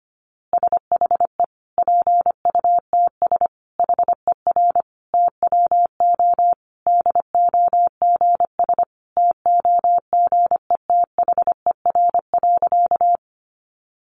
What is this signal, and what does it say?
Morse code, heard as a keyed tone, about 25 wpm, 700 Hz. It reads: SHE PUTS HER TWO DOGS TOGETHER.
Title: Easy Phrases 25wpm